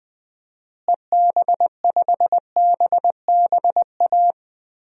Morse code, heard as a keyed tone, about 20 wpm, 700 Hz. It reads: EB5BBA
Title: DXCC-100 Call Signs - 6 Characters - Repeated 3x - 20wpm